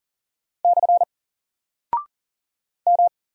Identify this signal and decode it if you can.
Morse code, heard as a keyed tone, about 39 wpm, 700 Hz. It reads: / M
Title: Single Character (Letter:Number:Punctuation:Prosign) 40wpm